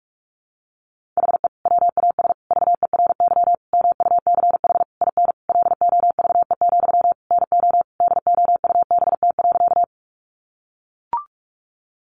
Morse code, heard as a keyed tone, about 44 wpm, 700 Hz. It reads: HE WAS VERY MUCH IN LOVE, NO DOUBT.
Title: Sentences from Top 1500 Words 45wpm